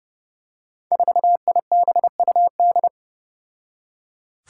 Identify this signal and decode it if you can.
Morse code, heard as a keyed tone, about 30 wpm, 700 Hz. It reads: 4IBUD
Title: Five Challenging Letter-Number Combinations 30wpm